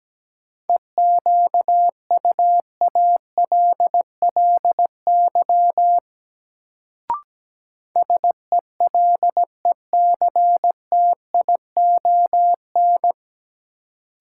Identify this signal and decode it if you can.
Morse code, heard as a keyed tone, about 17 wpm, 700 Hz. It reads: EQUALLY SELECTION
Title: Sentences from Top 1400 Words - New Word Drill Alt - 17wpm